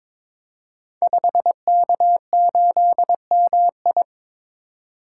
Morse code, heard as a keyed tone, about 22 wpm, 700 Hz. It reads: 5K8MI
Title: Code Groups - 5 Letter:Numbers - 22wpm